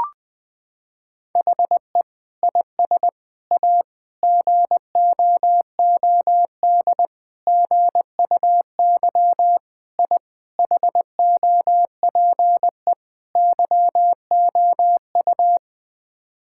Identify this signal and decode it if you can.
Morse code, heard as a keyed tone, about 20 wpm, 700 Hz. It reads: HE IS A GOOD GUY I HOPE YOU
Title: Sets of 8 Words - 20wpm